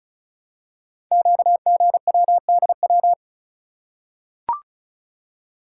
Morse code, heard as a keyed tone, about 36 wpm, 700 Hz. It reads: QGWDW
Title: Code Groups - 5 Letters - 35wpm